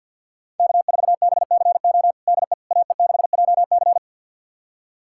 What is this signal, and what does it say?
Morse code, heard as a keyed tone, about 47 wpm, 700 Hz. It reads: K4BCO DE AE6JC
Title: USA Callsign DE Callsign - 5 Characters - 50wpm